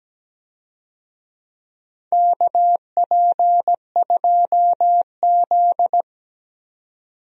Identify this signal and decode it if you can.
Morse code, heard as a keyed tone, about 17 wpm, 700 Hz. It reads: KP2Z
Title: DXCC-100 Call Signs - 4 Characters - 17wpm